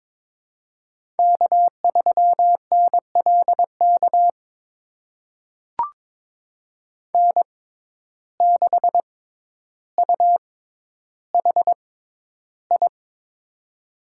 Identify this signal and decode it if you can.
Morse code, heard as a keyed tone, about 22 wpm, spaced as for 11 wpm, 700 Hz. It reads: K3NLK N6UHI
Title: USA Call Signs - 6x Character Spacing - 22wpm